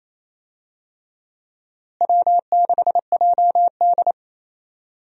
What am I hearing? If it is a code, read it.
Morse code, heard as a keyed tone, about 28 wpm, 700 Hz. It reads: W6JD
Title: USA Call Signs - 4 Characters - Repeated 3x - 28wpm